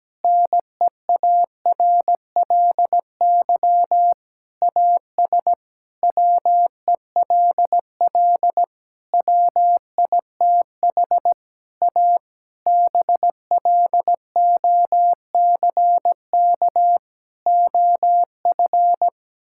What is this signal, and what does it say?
Morse code, heard as a keyed tone, about 17 wpm, 700 Hz. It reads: NEARLY AS WELL WITH A BLOCK OF
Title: Sentences from Top 1600 Words 17wpm